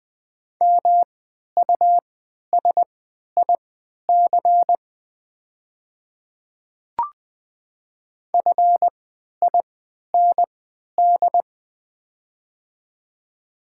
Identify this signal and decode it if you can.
Morse code, heard as a keyed tone, about 21 wpm, spaced as for 11 wpm, 700 Hz. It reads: MUSIC FIND
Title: 3x Character Spacing - Top 300 Words - 20wpm